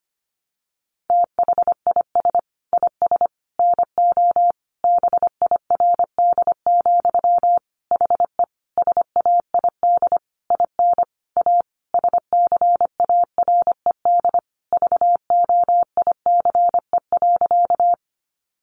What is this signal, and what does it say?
Morse code, heard as a keyed tone, about 25 wpm, 700 Hz. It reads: THIS IS NO BIRD, HE SAID IN A SCARED VOICE.
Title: Sentences from Top 700 Words 25wpm